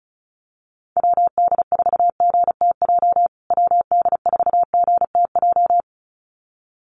Morse code, heard as a keyed tone, about 34 wpm, 700 Hz. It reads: WD4GTJ WD4GTJ
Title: USA Call Signs Repeated 2x 35wpm